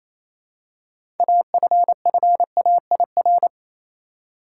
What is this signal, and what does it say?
Morse code, heard as a keyed tone, about 28 wpm, 700 Hz. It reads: AFFAIR
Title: Six Letter Words - 28wpm